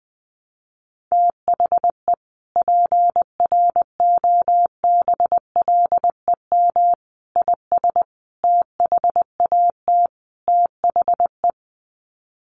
Morse code, heard as a keyed tone, about 20 wpm, 700 Hz. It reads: THE PROBLEM IS THAT THE
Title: Sets of 5 Words Encore2 20wpm